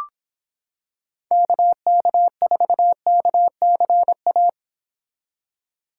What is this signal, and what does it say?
Morse code, heard as a keyed tone, about 26 wpm, 700 Hz. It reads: KK4KCA